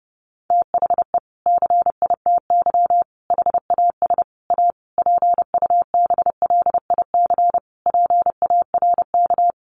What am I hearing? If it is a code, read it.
Morse code, heard as a keyed tone, about 30 wpm, 700 Hz. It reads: THE CITY HAS A PUBLIC PARK
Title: Sentences from Top 1300 Words 30wpm